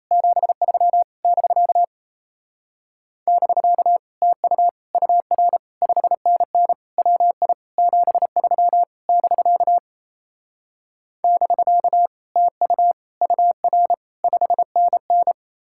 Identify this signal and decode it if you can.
Morse code, heard as a keyed tone, about 37 wpm, 700 Hz. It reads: 73 <BK> <BK> TU UR 5NN WI 73 <BK> <BK> TU UR 5NN